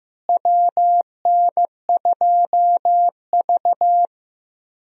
Morse code, heard as a keyed tone, about 15 wpm, 700 Hz. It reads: WN2V
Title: USA Call Signs - 4 Characters - Repeated 3x - 15wpm